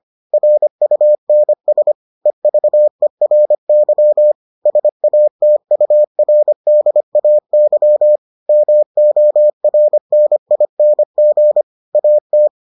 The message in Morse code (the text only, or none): RUNS EVERY SATURDAY MORNING AT